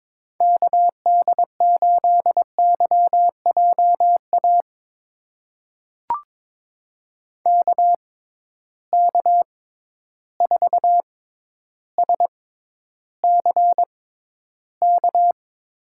Morse code, heard as a keyed tone, about 22 wpm, spaced as for 11 wpm, 700 Hz. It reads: KD8YJA KK4SCK
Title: USA Call Signs - 6 Characters - 6x Character Spacing - 22wpm